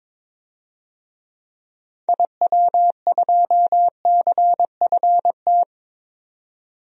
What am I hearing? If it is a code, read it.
Morse code, heard as a keyed tone, about 22 wpm, 700 Hz. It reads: IW2CFT
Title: DXCC-100 Call Signs - 6 Characters - 22wpm